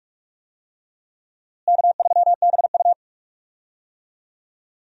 Morse code, heard as a keyed tone, about 46 wpm, 700 Hz. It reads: K3BU
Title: USA Call Signs - 4 Characters - Repeated 3x - 45wpm